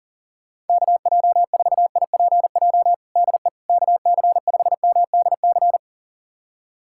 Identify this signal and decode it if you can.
Morse code, heard as a keyed tone, about 40 wpm, 700 Hz. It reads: KJ4IPJ DE KC5MDC
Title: USA Callsign DE Callsign - 6 Characters - 40wpm